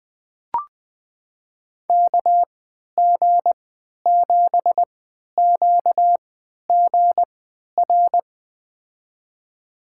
Morse code, heard as a keyed tone, about 20 wpm, spaced as for 11 wpm, 700 Hz. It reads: KG7QGR
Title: USA Call Signs - 6 Characters - 3x Character Spacing - 20wpm